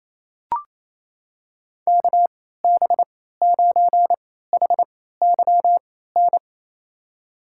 Morse code, heard as a keyed tone, about 28 wpm, spaced as for 16 wpm, 700 Hz. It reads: KB9HYN
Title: USA Call Signs - 6 Characters - 3x Character Spacing - 28wpm